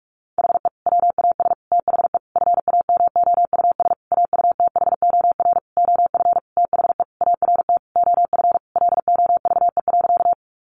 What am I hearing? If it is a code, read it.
Morse code, heard as a keyed tone, about 44 wpm, 700 Hz. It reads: HE WAS THE FAMOUS AUTHOR OF THE ART OF LOVE.
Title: Sentences from Top 1500 Words 45wpm